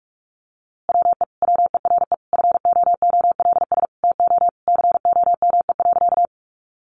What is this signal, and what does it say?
Morse code, heard as a keyed tone, about 44 wpm, 700 Hz. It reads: WE WERE FOOLS TO COME.
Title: A Sound Of Thunder 45wpm